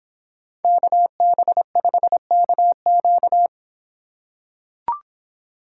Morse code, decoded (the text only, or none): KB5KQ